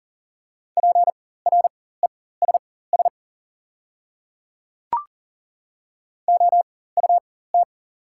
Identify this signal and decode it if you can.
Morse code, heard as a keyed tone, about 40 wpm, spaced as for 19 wpm, 690 Hz. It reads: PRESS OUT
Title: 4x Character Spacing - Top 300 Words - 40wpm